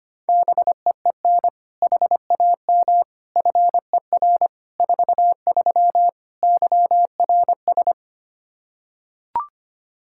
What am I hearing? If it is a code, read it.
Morse code, heard as a keyed tone, about 25 wpm, 700 Hz. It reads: BEEN HAM FER 43 YRS
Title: Practice QSO - Exchange 3 - More Info - 25wpm